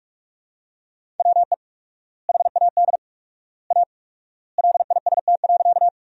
Morse code, heard as a keyed tone, about 46 wpm, 700 Hz. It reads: WE HAD A LIST.
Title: 4x Word-Spacing - Sentences from Top 400 Words - 45wpm